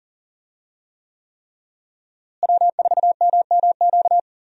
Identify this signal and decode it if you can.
Morse code, heard as a keyed tone, about 37 wpm, 700 Hz. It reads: W4MMQ
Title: USA Call Signs - 5 Characters - Repeated 3x - 40wpm